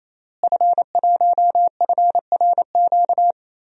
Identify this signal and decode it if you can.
Morse code, heard as a keyed tone, about 28 wpm, 700 Hz. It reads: F1FRQ
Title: DXCC-100 Call Signs - 5 Characters - Repeated 3x - 28wpm